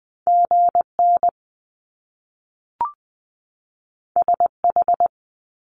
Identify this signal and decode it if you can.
Morse code, heard as a keyed tone, about 20 wpm, 700 Hz. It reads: GN SH
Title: Top Two Letter Combinations in Words 20wpm